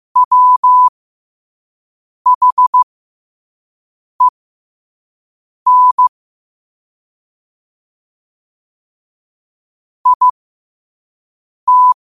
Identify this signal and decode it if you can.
Morse code, heard as a keyed tone, about 15 wpm, spaced as for 5 wpm, 1000 Hz. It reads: WHEN IT
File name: qotdfiles / 5wpm